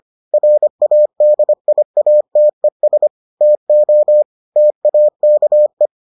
Morse code, heard as a keyed tone, about 25 wpm, 600 Hz. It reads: RADIATES TO TAKE